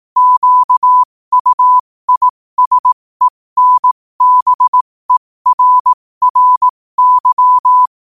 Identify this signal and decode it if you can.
Morse code, heard as a keyed tone, about 18 wpm, spaced as for 15 wpm, 1000 Hz. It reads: QUISENBERRY
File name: qotdfiles / 15wpm